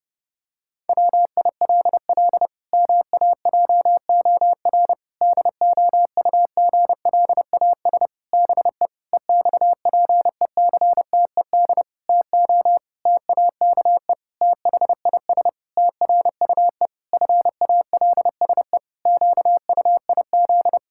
Morse code, heard as a keyed tone, about 30 wpm, 700 Hz. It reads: WILL MAJOR DOUGLAS BE EXPECTED TO TAKE THIS TRUE FALSE QUIZ
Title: Pangram Challenge v2 - 30wpm